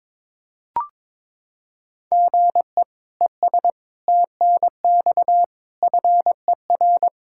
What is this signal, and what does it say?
Morse code, heard as a keyed tone, about 22 wpm, 700 Hz. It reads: GE ES TNX FER
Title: Practice QSO - Exchange 1 - Beginning - 22wpm